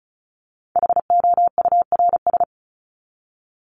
Morse code, heard as a keyed tone, about 34 wpm, 700 Hz. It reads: HOURS
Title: ES:S Suffix - Top 100 Words with Base Words 35wpm